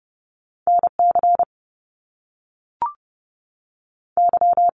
Morse code, heard as a keyed tone, about 30 wpm, 700 Hz. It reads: NC Y
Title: Top Two Letter Combinations in Words 30wpm